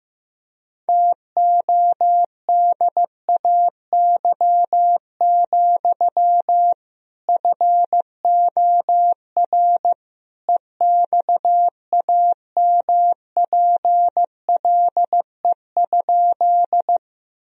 Morse code, heard as a keyed tone, about 16 wpm, 700 Hz. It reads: TODAY, FOR EXAMPLE?
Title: Sentences from Top 1000 Words - Review Part 1 - 15wpm